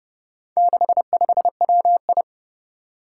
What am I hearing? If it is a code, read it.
Morse code, heard as a keyed tone, about 30 wpm, 700 Hz. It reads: 65WI
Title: Four Challenging Letter-Number Combinations 30wpm